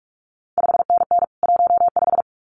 Morse code, heard as a keyed tone, about 46 wpm, 700 Hz. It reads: 5NN 15
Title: CQ WW DX Contest Exchange - 45wpm